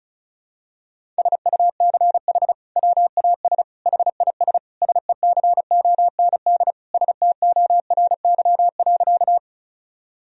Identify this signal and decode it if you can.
Morse code, heard as a keyed tone, about 34 wpm, 700 Hz. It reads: SUCH WAS HIS SECOND STORY.
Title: Sentences from Top 1000 Words - Review Part 1 - 35wpm